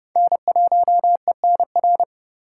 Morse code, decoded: N1ENR

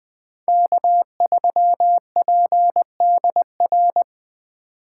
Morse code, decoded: K3PDR

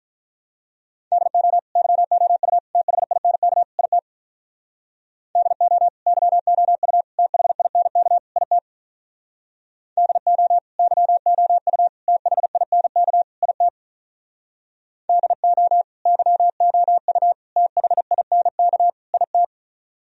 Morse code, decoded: DO YOU THINK IT DO YOU THINK IT DO YOU THINK IT DO YOU THINK IT

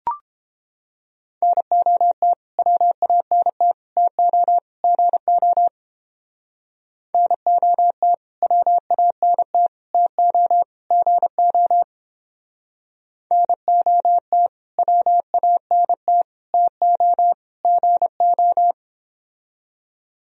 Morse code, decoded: NOT WANT TO GO NOT WANT TO GO NOT WANT TO GO

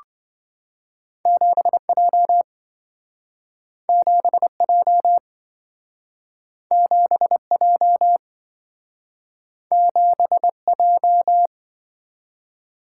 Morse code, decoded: 7J 7J 7J 7J

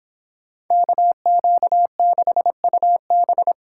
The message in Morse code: KQ6UB

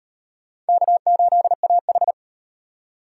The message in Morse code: K8AH

